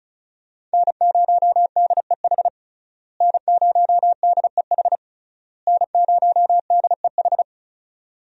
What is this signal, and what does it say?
Morse code, heard as a keyed tone, about 36 wpm, 700 Hz. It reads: N0DEH N0DEH N0DEH
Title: USA Call Signs - 5 Characters - Repeated 3x - 35wpm